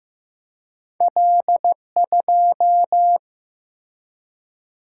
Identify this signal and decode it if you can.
Morse code, heard as a keyed tone, about 15 wpm, 700 Hz. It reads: L2
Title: Single-Letter Single-Number 15wpm